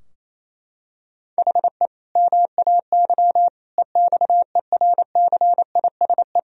Morse code, heard as a keyed tone, about 28 wpm, 700 Hz. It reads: HE MAY EXERCISE
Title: Sentences from Top 1000 Words - Review Part 4 - 28wpm